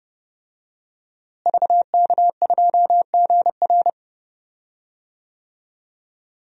Morse code, heard as a keyed tone, about 32 wpm, 700 Hz. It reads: VK2GR
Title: DXCC-100 Call Signs - 5 Characters - Repeated 3x - 30wpm